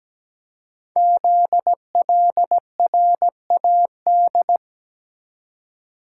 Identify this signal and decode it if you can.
Morse code, heard as a keyed tone, about 17 wpm, 700 Hz. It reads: ZLRAD
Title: Code Groups - 5 Letter:Numbers - 17wpm